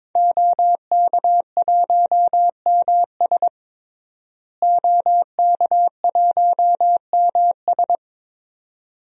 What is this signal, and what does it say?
Morse code, heard as a keyed tone, about 22 wpm, 700 Hz. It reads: OK1MS OK1MS
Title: DXCC-100 Call Signs - 5 Characters - Repeated 2x - 22wpm